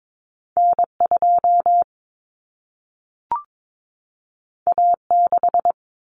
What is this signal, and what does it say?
Morse code, heard as a keyed tone, about 22 wpm, 700 Hz. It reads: N2 A6
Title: Common Cut Numbers - 2-Digit 22wpm